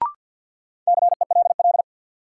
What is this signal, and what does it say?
Morse code, heard as a keyed tone, about 49 wpm, 700 Hz. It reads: CELL
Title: Sentences from Top 1100 Words - New Word Drill - 50wpm